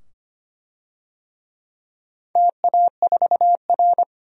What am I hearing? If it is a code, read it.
Morse code, heard as a keyed tone, about 25 wpm, 700 Hz. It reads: TA4R
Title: DXCC-100 Call Signs - 4 Characters - 25wpm